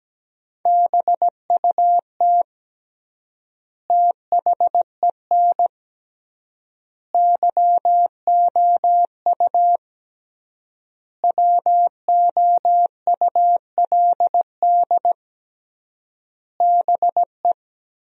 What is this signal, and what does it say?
Morse code, heard as a keyed tone, about 17 wpm, 700 Hz. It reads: BUT THEN YOU WOULD BE
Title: 3x Word-Spacing - Sentences from Top 300 Words - 17wpm